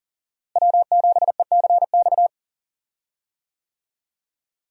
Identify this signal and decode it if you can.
Morse code, heard as a keyed tone, about 40 wpm, 700 Hz. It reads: W7ECX
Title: USA Call Signs - 5 Characters - Repeated 2x - 40wpm